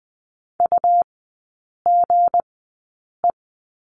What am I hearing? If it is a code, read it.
Morse code, heard as a keyed tone, about 20 wpm, spaced as for 8 wpm, 700 Hz. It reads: UGE
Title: 166 Lesson 27e - Number 7 - Three Characters 20wpm